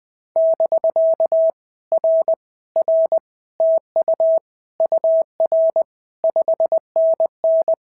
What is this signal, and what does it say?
Morse code, heard as a keyed tone, about 20 wpm, 650 Hz. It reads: <BK> R R TU UR 5NN
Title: POTA QSO Practice - 20wpm